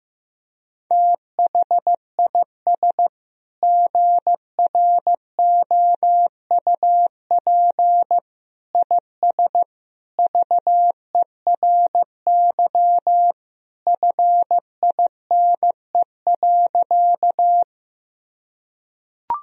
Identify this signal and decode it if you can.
Morse code, heard as a keyed tone, about 14 wpm, 700 Hz. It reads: THIS GROUP IS VERY FINE.
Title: Sentences from Top 1000 Words - Review Part 1 - 15wpm